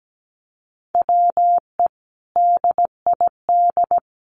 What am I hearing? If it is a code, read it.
Morse code, heard as a keyed tone, about 17 wpm, 700 Hz. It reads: WE DID
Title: Sentences from Top 1100 Words 17wpm